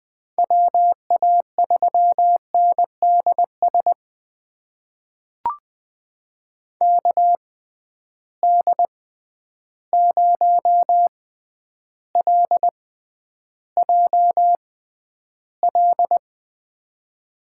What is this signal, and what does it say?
Morse code, heard as a keyed tone, about 20 wpm, spaced as for 10 wpm, 700 Hz. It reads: WA3NDS KD0LJL
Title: USA Call Signs - 6 Characters - 6x Character Spacing - 20wpm